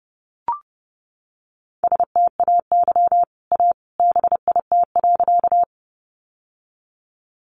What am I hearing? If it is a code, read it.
Morse code, heard as a keyed tone, about 30 wpm, 700 Hz. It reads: STAY A BIT.
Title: Sentences from Top 900 Words 30wpm